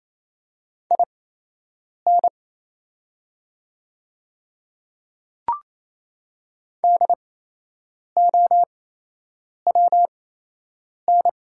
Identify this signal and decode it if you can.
Morse code, heard as a keyed tone, about 28 wpm, spaced as for 8 wpm, 700 Hz. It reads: IN DOWN
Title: 8x Character Spacing - Top 200 Words - 28wpm